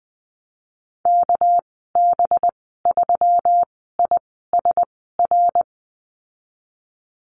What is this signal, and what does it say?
Morse code, heard as a keyed tone, about 20 wpm, spaced as for 14 wpm, 700 Hz. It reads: KB3ISR
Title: 207 Lesson 33g - Letter j - Call Signs 20wpm